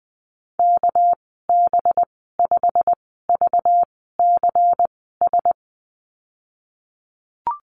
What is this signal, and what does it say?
Morse code, heard as a keyed tone, about 20 wpm, spaced as for 15 wpm, 700 Hz. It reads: KB5VCS
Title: 231 Lesson 37g - Letter x - Call Signs 20wpm